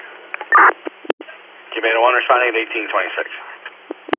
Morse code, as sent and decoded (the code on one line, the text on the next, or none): none